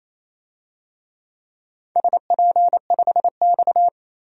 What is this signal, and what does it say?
Morse code, heard as a keyed tone, about 28 wpm, 700 Hz. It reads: SP5X
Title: DXCC-100 Call Signs - 4 Characters - Repeated 2x - 28wpm